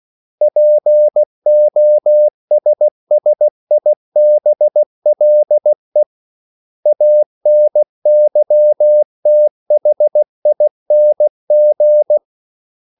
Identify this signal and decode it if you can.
Morse code, heard as a keyed tone, about 16 wpm, 600 Hz. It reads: POSSIBLE ANYTHING
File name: CW / Practice / 500MostCommonEnglishWords16